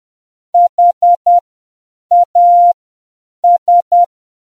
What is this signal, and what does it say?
Morse code, heard as a keyed tone, about 10 wpm, spaced as for 7 wpm, 700 Hz. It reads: HAS